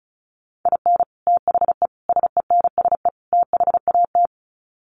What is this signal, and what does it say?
Morse code, heard as a keyed tone, about 34 wpm, 700 Hz. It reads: IN THE SENSE THAT
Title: Sets of 4 Words 35wpm